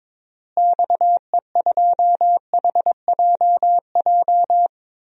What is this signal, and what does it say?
Morse code, heard as a keyed tone, about 22 wpm, 700 Hz. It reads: XE2HJJ
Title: DXCC-100 Call Signs - 6 Characters - Repeated 3x - 22wpm